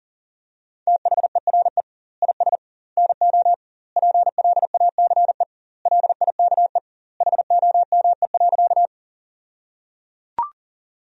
Morse code, heard as a keyed tone, about 37 wpm, 700 Hz. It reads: THERE IS NO PLACE LIKE HOME.
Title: 2x Word-Spacing - Sentences from Top 200 Words - 40wpm